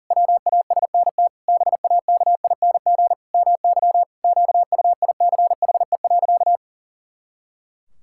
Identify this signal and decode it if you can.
Morse code, heard as a keyed tone, about 40 wpm, 700 Hz. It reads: WASNT BAKING MY QUICHE.
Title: Pangram Challenge v2 - 40wpm